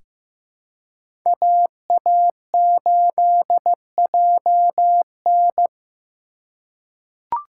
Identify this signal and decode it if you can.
Morse code, heard as a keyed tone, about 15 wpm, 700 Hz. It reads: AA8JN
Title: USA Call Signs - 5 Characters - 3x Character Spacing - 15wpm